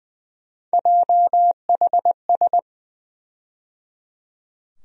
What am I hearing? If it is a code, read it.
Morse code, heard as a keyed tone, about 20 wpm, 700 Hz. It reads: JHS
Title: USA Call Sign Postfixes - 3 Letters 20wpm